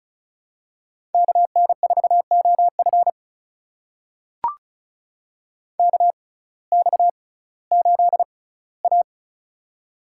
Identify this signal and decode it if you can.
Morse code, heard as a keyed tone, about 36 wpm, spaced as for 19 wpm, 700 Hz. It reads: KN4OF KX8A
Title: USA Call Signs - 6x Character Spacing - 35wpm